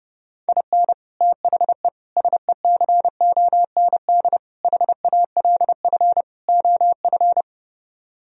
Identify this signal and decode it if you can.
Morse code, heard as a keyed tone, about 30 wpm, 700 Hz. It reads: IN THE SECOND HALF OF